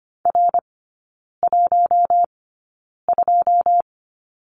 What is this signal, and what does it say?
Morse code, heard as a keyed tone, about 25 wpm, spaced as for 9 wpm, 700 Hz. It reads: R12
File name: Lesson 14e - Number 2 - Three Characters 25wpm